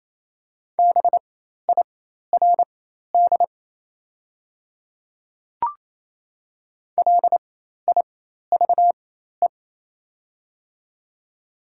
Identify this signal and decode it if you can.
Morse code, heard as a keyed tone, about 28 wpm, spaced as for 13 wpm, 700 Hz. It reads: BIRD LIVE
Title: 4x Character Spacing - Top 500 Words - 28wpm